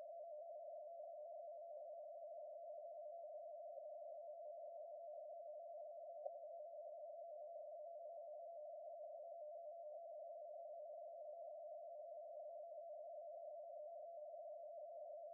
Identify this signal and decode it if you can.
Two-tone teleprinter signal, about 80 Hz shift.